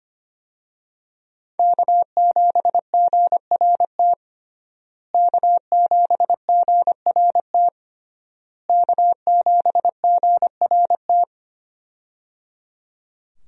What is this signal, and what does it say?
Morse code, heard as a keyed tone, about 25 wpm, 700 Hz. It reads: K7GRT K7GRT K7GRT
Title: USA Call Signs - 5 Characters - Repeated 3x - 25wpm